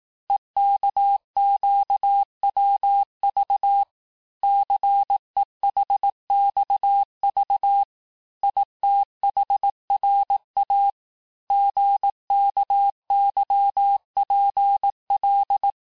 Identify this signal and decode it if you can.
Morse code, heard as a keyed tone, about 18 wpm, 800 Hz. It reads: EKQWV CEHXV ITHRA GKYPL
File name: cours-du-lundi files / cours-1594--teste-chiffre-vitesse-17mpm